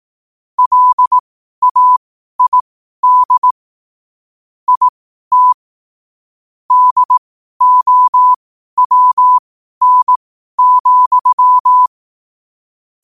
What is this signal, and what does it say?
Morse code, heard as a keyed tone, about 18 wpm, spaced as for 12 wpm, 1000 Hz. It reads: LAID IT DOWN,
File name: qotdfiles / 12wpm